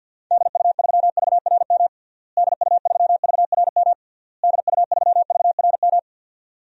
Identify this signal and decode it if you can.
Morse code, heard as a keyed tone, about 51 wpm, 700 Hz. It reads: DU3VRM DU3VRM DU3VRM
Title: DXCC-100 Call Signs - 6 Characters - Repeated 3x - 50wpm